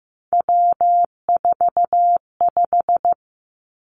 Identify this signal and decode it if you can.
Morse code, heard as a keyed tone, about 15 wpm, 700 Hz. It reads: W45
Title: Three Challenging Letter-Number Combinations 15wpm